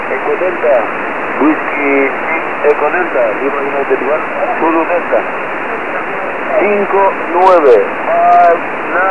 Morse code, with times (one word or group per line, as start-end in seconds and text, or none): none